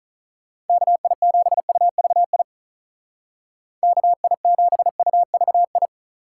Morse code, heard as a keyed tone, about 45 wpm, 700 Hz. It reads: KI7UVI KI7UVI